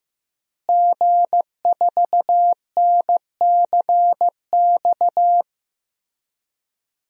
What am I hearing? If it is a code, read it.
Morse code, heard as a keyed tone, about 15 wpm, 700 Hz. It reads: G4NCX
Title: DXCC-100 Call Signs - 5 Characters - Repeated 2x - 15wpm